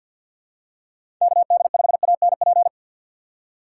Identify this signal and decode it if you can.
Morse code, heard as a keyed tone, about 51 wpm, 700 Hz. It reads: KD5ANP
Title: USA Call Signs - 6 Characters - 3x Character Spacing - 50wpm